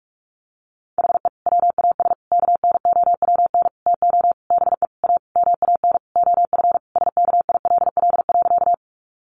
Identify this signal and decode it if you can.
Morse code, heard as a keyed tone, about 44 wpm, 700 Hz. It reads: HE WAS KNOWN TO BE A MAN OF SKILL.
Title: Sentences from Top 1200 Words 45wpm